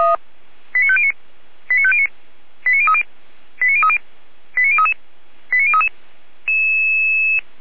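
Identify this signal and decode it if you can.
Five-tone selective call (ZVEI), heard as quick runs of five tones, tones of about 70 ms. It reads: 89509 89509 89030 89030 89033 89033
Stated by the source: ZVEI five-tone selective call received on an SDR by Feuerwehr Orlamünde.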